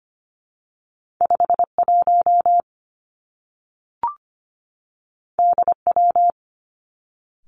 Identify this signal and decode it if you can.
Morse code, heard as a keyed tone, about 25 wpm, 700 Hz. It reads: 5 1 D W
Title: Two Challenging Letter-Number Combinations 25wpm